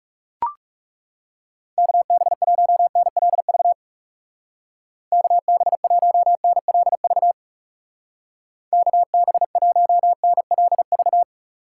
Morse code, heard as a keyed tone, about 40 wpm, 700 Hz. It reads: KB1NLV KB1NLV KB1NLV